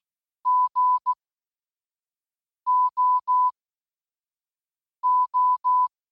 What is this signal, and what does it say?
Morse code, heard as a keyed tone, about 16 wpm, 1000 Hz. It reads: G O O